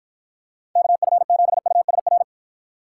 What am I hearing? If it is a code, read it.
Morse code, heard as a keyed tone, about 54 wpm, 700 Hz. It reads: KF7USR